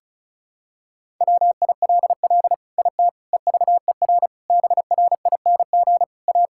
Morse code, heard as a keyed tone, about 34 wpm, 700 Hz. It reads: WILL IT EVER BRING A
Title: Sentences from Top 1000 Words - Review Part 5 - 35wpm